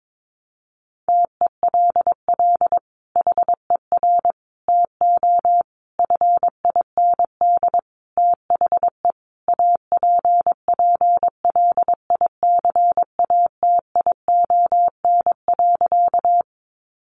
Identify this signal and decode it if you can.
Morse code, heard as a keyed tone, about 22 wpm, 700 Hz. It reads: TELL HER TO FIND THE APPLICATION.
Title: Sentences from Top 1200 Words 22wpm